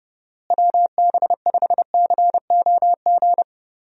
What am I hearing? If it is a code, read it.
Morse code, heard as a keyed tone, about 32 wpm, 700 Hz. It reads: WB5COG
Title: USA Call Signs - 6 Characters - Repeated 3x - 30wpm